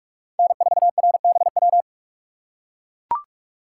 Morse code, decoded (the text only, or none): N4RBW